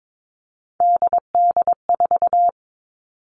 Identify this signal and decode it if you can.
Morse code, heard as a keyed tone, about 22 wpm, 700 Hz. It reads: DD4
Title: Three Challenging Letter-Number Combinations 22wpm